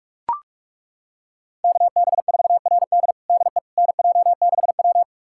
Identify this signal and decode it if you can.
Morse code, heard as a keyed tone, about 42 wpm, 700 Hz. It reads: KB4RD DE NJ6W
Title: USA Callsign DE Callsign 45wpm